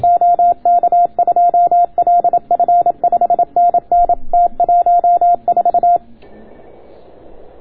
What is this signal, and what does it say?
Morse code, heard as a keyed tone, about 27 wpm, 690 Hz. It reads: OK2LF 5NN T14